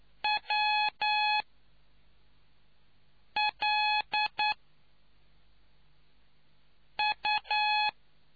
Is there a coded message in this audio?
Yes: Morse code from 0.2 to 7.9 s, heard as a keyed tone, about 9 wpm, 840 Hz. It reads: W L U